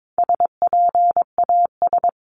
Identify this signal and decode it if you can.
Morse code, heard as a keyed tone, about 22 wpm, 700 Hz. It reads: SPAS
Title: CWT Exchange 22wpm